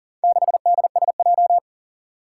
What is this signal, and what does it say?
Morse code, heard as a keyed tone, about 40 wpm, 700 Hz. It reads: 6DSJ
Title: Four Challenging Letter-Number Combinations 40wpm